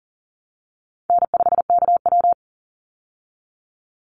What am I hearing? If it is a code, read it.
Morse code, heard as a keyed tone, about 39 wpm, 700 Hz. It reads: N5KW
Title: Easy Call Signs Repeated 2x - 40wpm